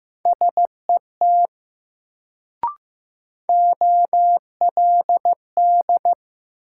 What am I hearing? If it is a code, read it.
Morse code, heard as a keyed tone, about 15 wpm, 700 Hz. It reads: SET OLD
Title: Sentences from Top 200 Words - New Word Drill - 15wpm